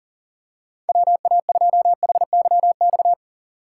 Morse code, heard as a keyed tone, about 40 wpm, 700 Hz. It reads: WA2HYX
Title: USA Call Signs - 6 Characters - 4x Character Spacing - 40wpm